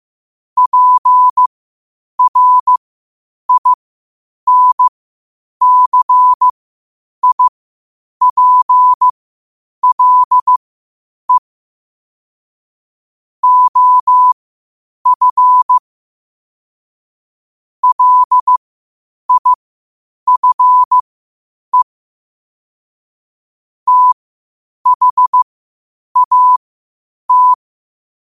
Morse code, heard as a keyed tone, about 15 wpm, spaced as for 8 wpm, 1000 Hz. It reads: PRINCIPLE OF LIFE THAT
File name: qotdfiles / 8wpm